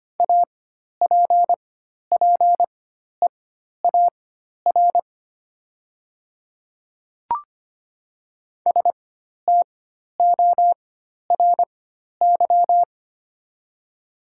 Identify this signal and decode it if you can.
Morse code, heard as a keyed tone, about 25 wpm, spaced as for 12 wpm, 700 Hz. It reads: APPEAR STORY
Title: 4x Character Spacing - Top 500 Words - 25wpm